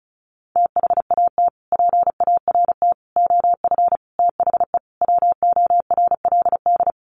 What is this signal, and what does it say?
Morse code, heard as a keyed tone, about 34 wpm, 700 Hz. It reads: THAT PART OF THE WORLD
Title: Sets of 5 Words Encore2 35wpm